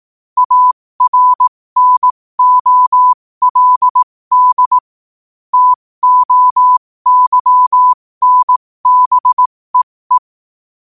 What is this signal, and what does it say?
Morse code, heard as a keyed tone, about 18 wpm, spaced as for 15 wpm, 1000 Hz. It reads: ARNOLD TOYNBEE